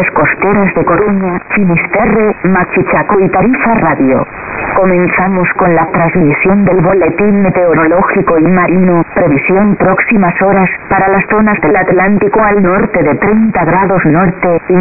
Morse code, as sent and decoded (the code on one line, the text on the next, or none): none